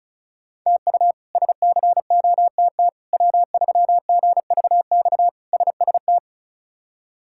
Morse code, heard as a keyed tone, about 36 wpm, 700 Hz. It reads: TU SCOTT W3GVX SST
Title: K1USN SST QSO Practice - 35wpm